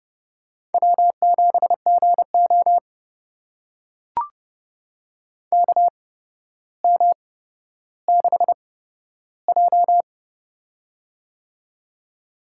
Morse code, heard as a keyed tone, about 31 wpm, spaced as for 13 wpm, 700 Hz. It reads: W7GO KM6J
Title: USA Call Signs - 8x Character Spacing - 30wpm